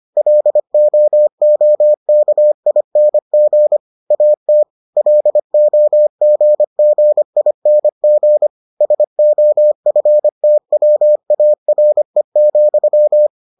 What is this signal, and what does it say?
Morse code, heard as a keyed tone, about 25 wpm, 600 Hz. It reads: LOOKING AT LOGGING SOFTWARE,